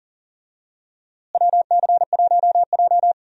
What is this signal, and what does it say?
Morse code, heard as a keyed tone, about 40 wpm, 700 Hz. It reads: WC1J
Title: USA Call Signs - 4 Characters - Repeated 3x - 40wpm